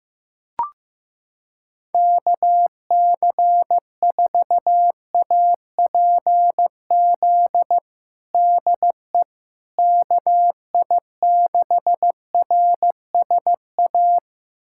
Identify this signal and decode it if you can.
Morse code, heard as a keyed tone, about 16 wpm, 700 Hz. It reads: KC4APZ DE KI6RSA
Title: USA Callsign DE Callsign - 6 Characters - 15wpm